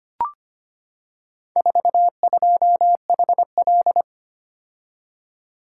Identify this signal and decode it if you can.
Morse code, heard as a keyed tone, about 25 wpm, 700 Hz. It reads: 42HL
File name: practice / 25wpm / Four Letter-Number Combinations - 25wpm